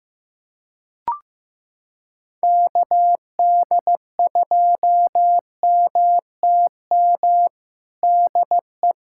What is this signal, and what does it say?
Morse code, heard as a keyed tone, about 15 wpm, 700 Hz. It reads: KD2MTM DE
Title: USA Callsign DE Callsign - 6 Characters - 15wpm